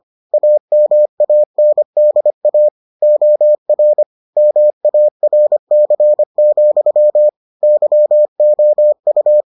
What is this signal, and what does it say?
Morse code, heard as a keyed tone, about 25 wpm, 600 Hz. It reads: AMANDA OR MARC, YOU